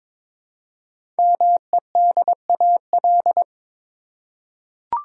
Morse code, heard as a keyed tone, about 22 wpm, 700 Hz. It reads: MEDAL
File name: 5 Letter Words - First 3 Letters Given - 22wpm